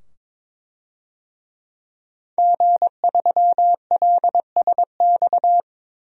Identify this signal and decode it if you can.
Morse code, heard as a keyed tone, about 22 wpm, 700 Hz. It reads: G3LSX
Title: DXCC-100 Call Signs - 5 Characters - 22wpm